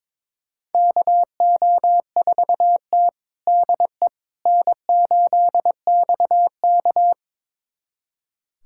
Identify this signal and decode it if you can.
Morse code, heard as a keyed tone, about 22 wpm, 700 Hz. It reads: KO4T DE N8XK
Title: USA Callsign DE Callsign - 4 Characters - 22wpm